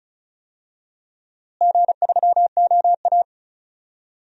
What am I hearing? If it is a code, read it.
Morse code, heard as a keyed tone, about 36 wpm, 700 Hz. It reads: G3OA
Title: DXCC-100 Call Signs - 4 Characters - Repeated 3x - 35wpm